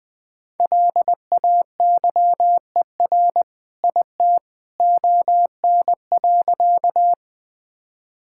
Morse code, heard as a keyed tone, about 20 wpm, 700 Hz. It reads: LAYER IT ON.
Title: Sentences from Top 1600 Words 20wpm